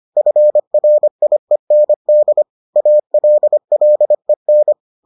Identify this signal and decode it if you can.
Morse code, heard as a keyed tone, about 25 wpm, 600 Hz. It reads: FRIEND ALLEN